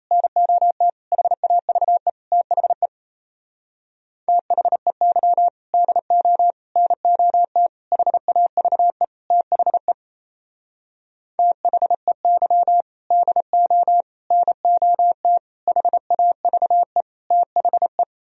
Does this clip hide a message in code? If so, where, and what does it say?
Yes: Morse code from 0.1 to 18.0 s, heard as a keyed tone, about 36 wpm, 700 Hz. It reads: NOT HAVE THE THEY DO NOT HAVE THE THEY DO NOT HAVE THE